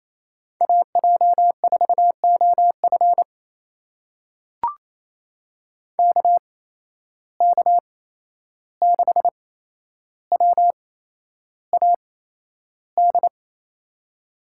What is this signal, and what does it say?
Morse code, heard as a keyed tone, about 28 wpm, spaced as for 11 wpm, 700 Hz. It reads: AJ4OF KK6WAD
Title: USA Call Signs - 8x Character Spacing - 28wpm